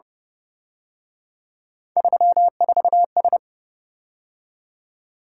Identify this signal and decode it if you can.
Morse code, heard as a keyed tone, about 28 wpm, 700 Hz. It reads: 34S